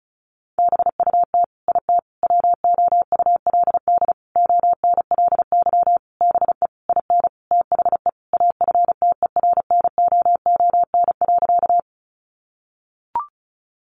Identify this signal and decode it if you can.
Morse code, heard as a keyed tone, about 34 wpm, 700 Hz. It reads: BUT IT WOULD ONLY BE IN THE AFTERNOON.
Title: Sentences from Top 1500 Words 35wpm